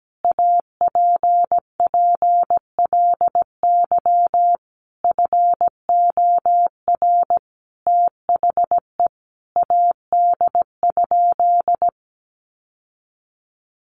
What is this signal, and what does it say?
Morse code, heard as a keyed tone, about 17 wpm, 700 Hz. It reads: APPLY FOR THE AD?
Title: Sentences from Top 1300 Words 17wpm